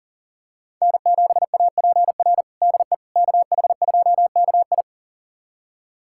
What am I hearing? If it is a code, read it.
Morse code, heard as a keyed tone, about 40 wpm, 700 Hz. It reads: N7APR DE KH2KI
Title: USA Callsign DE Callsign - 5 Characters - 40wpm